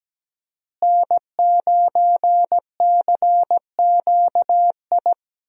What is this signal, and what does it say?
Morse code, heard as a keyed tone, about 17 wpm, 700 Hz. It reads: N9CQI
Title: USA Call Signs - 5 Characters - Repeated 2x - 17wpm